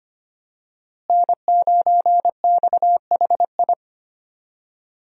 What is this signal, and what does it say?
Morse code, heard as a keyed tone, about 25 wpm, 700 Hz. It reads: N9XHI
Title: USA Call Signs - 5 Characters - Repeated 3x - 25wpm